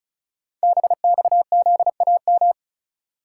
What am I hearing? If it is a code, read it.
Morse code, heard as a keyed tone, about 36 wpm, 700 Hz. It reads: BXZAM
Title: Code Groups - 5 Letters - 35wpm